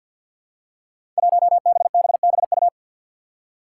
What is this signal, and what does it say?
Morse code, heard as a keyed tone, about 53 wpm, 700 Hz. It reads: 1BBBU
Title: Five Challenging Letter-Number Combinations 50wpm